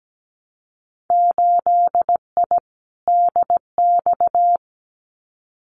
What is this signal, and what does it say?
Morse code, heard as a keyed tone, about 17 wpm, 700 Hz. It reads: 8I DX
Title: Winter Field Day Exchanges Encore 17wpm